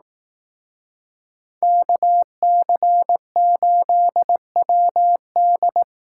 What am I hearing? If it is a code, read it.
Morse code, heard as a keyed tone, about 18 wpm, 700 Hz. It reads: KC8WD
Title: Speed Racing - USA 2x2 Call Signs 15wpm